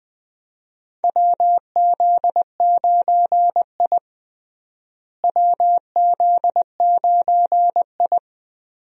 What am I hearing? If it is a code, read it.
Morse code, heard as a keyed tone, about 20 wpm, 700 Hz. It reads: WZ9I WZ9I
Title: USA Call Signs - 4 Characters - Repeated 3x - 20wpm